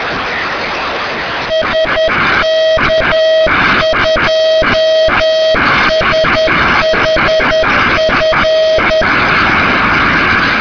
Morse code, heard as a keyed tone, about 10 wpm, 640 Hz. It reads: SK2SHF